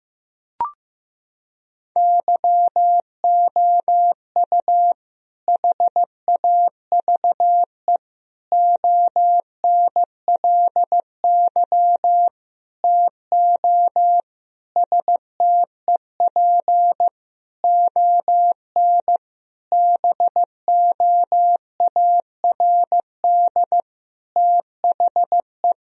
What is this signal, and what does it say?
Morse code, heard as a keyed tone, about 15 wpm, 700 Hz. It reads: YOU HAVE ONLY TO STEP ON BOARD THE
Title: Sentences from Top 1000 Words - Review Part 4 - 15wpm